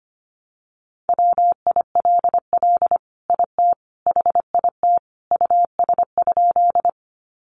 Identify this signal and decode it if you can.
Morse code, heard as a keyed tone, about 25 wpm, 700 Hz. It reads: WILL IT HIT US?
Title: Sentences from Top 1200 Words 25wpm